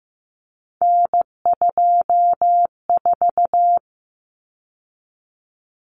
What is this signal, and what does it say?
Morse code, heard as a keyed tone, about 15 wpm, 700 Hz. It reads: N24
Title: Common Cut Numbers - 3-Digit 15wpm